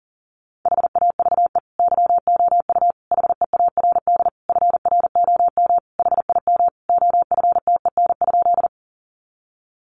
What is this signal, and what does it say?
Morse code, heard as a keyed tone, about 40 wpm, 700 Hz. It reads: HAVE YOU HEARD FROM HIM OFTEN?
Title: Sentences from Top 400 Words 40wpm